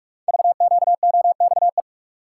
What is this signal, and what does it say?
Morse code, heard as a keyed tone, about 44 wpm, 700 Hz. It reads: VQOXE
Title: Code Groups - 5 Letters - 45wpm